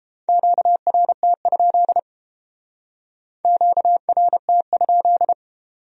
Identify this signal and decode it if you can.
Morse code, heard as a keyed tone, about 32 wpm, 700 Hz. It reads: QRT? QRT?